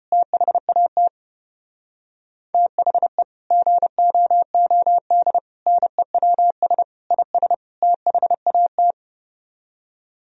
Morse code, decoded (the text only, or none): THAT THE GOOD NEWS IS THAT